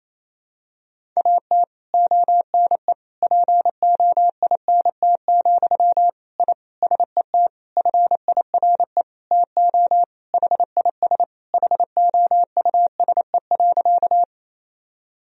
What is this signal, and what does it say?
Morse code, heard as a keyed tone, about 28 wpm, 700 Hz. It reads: AT ONE POINT, I SET FIRE TO HIS HOUSE.
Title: Sentences from Top 1000 Words - Review Part 1 - 28wpm